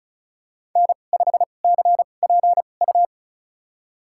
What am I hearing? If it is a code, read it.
Morse code, heard as a keyed tone, about 36 wpm, spaced as for 25 wpm, 700 Hz. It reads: N5CPU
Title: USA Call Signs - 5 Characters - 2x Character Spacing - 35wpm